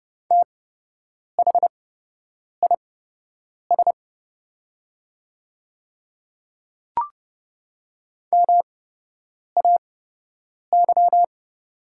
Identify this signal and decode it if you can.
Morse code, heard as a keyed tone, about 32 wpm, spaced as for 8 wpm, 700 Hz. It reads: THIS MAY
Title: 8x Character Spacing - Top 100 Words - 30wpm